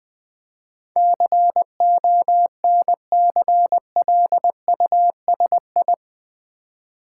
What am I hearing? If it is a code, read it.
Morse code, heard as a keyed tone, about 20 wpm, 700 Hz. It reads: CONCLUSI
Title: Build A Word - 4 Letters At A Time - 10-Letter Words - 20wpm